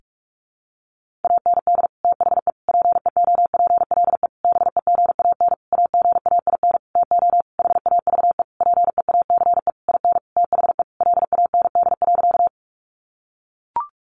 Morse code, heard as a keyed tone, about 44 wpm, 700 Hz. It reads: AND THE PEOPLE BEGAN AGAIN TO HAVE PEACE IN THE LAND.
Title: Sentences from Top 1500 Words 45wpm